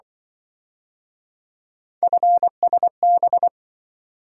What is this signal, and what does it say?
Morse code, heard as a keyed tone, about 25 wpm, 700 Hz. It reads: FSB
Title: Speed-Racing - Three Letter-Number Combinations - 20wpm